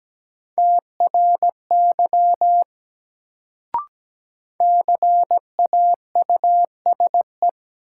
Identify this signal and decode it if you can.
Morse code, heard as a keyed tone, about 17 wpm, 700 Hz. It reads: TRY CAUSE
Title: Sentences from Top 200 Words - New Word Drill - 17wpm